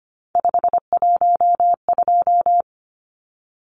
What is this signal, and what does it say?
Morse code, heard as a keyed tone, about 25 wpm, 700 Hz. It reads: 512
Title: Numbers - 3-Digit 25wpm